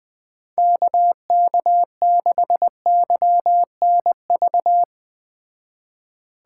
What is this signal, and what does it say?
Morse code, heard as a keyed tone, about 19 wpm, 700 Hz. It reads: KK6YNV
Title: Speed Racing - USA 2x3 Call Signs 17wpm